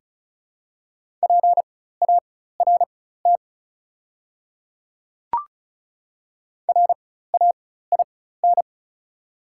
Morse code, heard as a keyed tone, about 36 wpm, spaced as for 16 wpm, 690 Hz. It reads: PART RAIN
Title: 4x Character Spacing - Top 500 Words - 35wpm